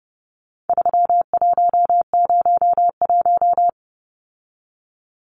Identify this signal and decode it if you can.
Morse code, heard as a keyed tone, about 30 wpm, 700 Hz. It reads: 3101
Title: Numbers - 4-Digit 30wpm